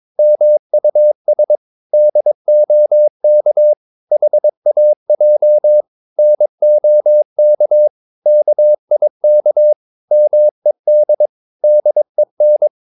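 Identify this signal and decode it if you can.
Morse code, heard as a keyed tone, about 22 wpm, 600 Hz. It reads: MUS DOK HAJ NOK KIK MED DEN
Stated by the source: Morse practice of short Danish words at 22 wpm, from CW med Gnister.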